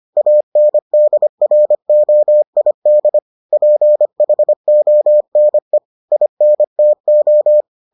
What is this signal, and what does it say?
Morse code, heard as a keyed tone, about 25 wpm, 600 Hz. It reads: ANDROID PHONE INTO